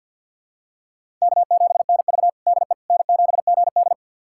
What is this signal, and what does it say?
Morse code, heard as a keyed tone, about 52 wpm, 700 Hz. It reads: K7NV DE N7GD
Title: USA Callsign DE Callsign - 4 Characters - 50wpm